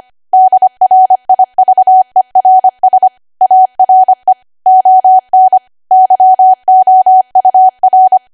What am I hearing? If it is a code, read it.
Morse code, heard as a keyed tone, about 25 wpm, 750 Hz. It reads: DRIVERS ARE ON YOUR
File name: cwa / W1AW-Level3 / 25WPM